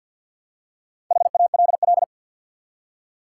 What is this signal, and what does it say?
Morse code, heard as a keyed tone, about 46 wpm, 700 Hz. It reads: HALL
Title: Four Letter Words - 50wpm